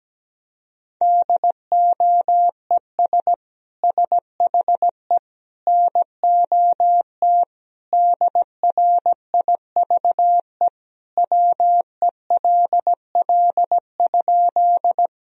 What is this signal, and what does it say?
Morse code, heard as a keyed tone, about 17 wpm, 700 Hz. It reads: DOES SHE NOT DRIVE WELL?
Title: Sentences from Top 1000 Words - Review Part 2 - 17wpm